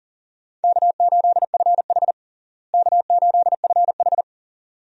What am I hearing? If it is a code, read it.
Morse code, heard as a keyed tone, about 37 wpm, 700 Hz. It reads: K8FH K8FH
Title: USA Call Signs - 4 Characters - Repeated 3x - 40wpm